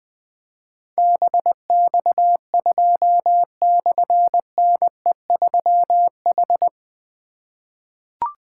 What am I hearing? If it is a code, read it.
Morse code, heard as a keyed tone, about 21 wpm, 700 Hz. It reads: BX2/NE3H
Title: Difficult Call Signs 20wpm